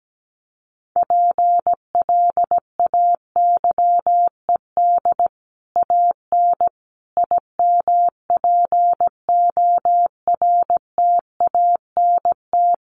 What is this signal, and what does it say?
Morse code, heard as a keyed tone, about 17 wpm, 700 Hz. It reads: PLAYED AN IMPORTANT
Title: Sets of 5 Words Encore2 17wpm